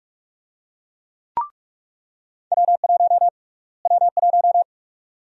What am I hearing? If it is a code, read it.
Morse code, heard as a keyed tone, about 42 wpm, 700 Hz. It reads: W1 W1
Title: USA Call Sign Prefixes - Repeated 2x - 45wpm